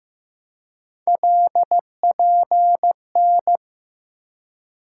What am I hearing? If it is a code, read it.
Morse code, heard as a keyed tone, about 15 wpm, 700 Hz. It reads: LPN
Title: K1USN SST QSO Practice 15wpm